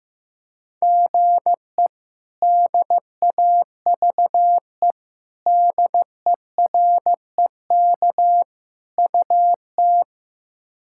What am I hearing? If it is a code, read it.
Morse code, heard as a keyed tone, about 15 wpm, 700 Hz. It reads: GE DAVE DEREK UT
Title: K1USN SST QSO Practice - 15wpm